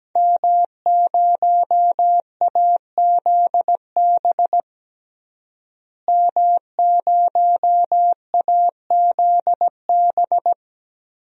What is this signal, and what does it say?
Morse code, heard as a keyed tone, about 17 wpm, 700 Hz. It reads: M0AZB M0AZB
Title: DXCC-100 Call Signs - 5 Characters - Repeated 2x - 17wpm